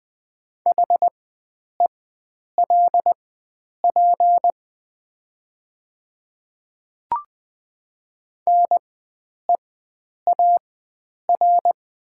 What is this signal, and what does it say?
Morse code, heard as a keyed tone, about 20 wpm, spaced as for 9 wpm, 700 Hz. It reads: HELP NEAR
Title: 4x Character Spacing - Top 500 Words - 20wpm